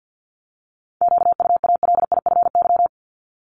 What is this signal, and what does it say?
Morse code, heard as a keyed tone, about 50 wpm, 700 Hz. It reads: QUALIFY
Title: Top 1751-2000 Words 50wpm